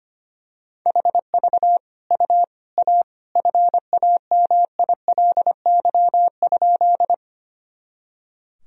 Morse code, heard as a keyed tone, about 25 wpm, 700 Hz. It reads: HV U A FAMILY?
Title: Sentences from Top 400 Words and Abbreviations 25wpm